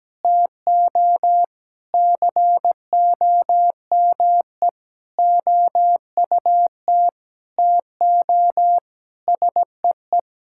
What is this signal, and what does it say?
Morse code, heard as a keyed tone, about 17 wpm, 700 Hz. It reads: TO COME OUT TO SEE
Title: Sentences from Top 1000 Words - Review Part 6 - 17wpm